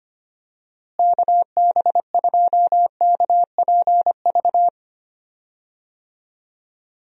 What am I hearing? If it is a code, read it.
Morse code, heard as a keyed tone, about 25 wpm, 700 Hz. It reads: KB2KPV
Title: USA Call Signs - 6 Characters - Repeated 3x - 25wpm